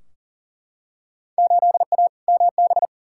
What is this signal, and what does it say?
Morse code, heard as a keyed tone, about 40 wpm, 700 Hz. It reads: 8A MB
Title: Field Day Exchanges Encore - 40wpm